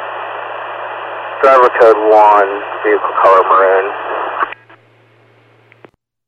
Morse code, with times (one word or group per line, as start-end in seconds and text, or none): none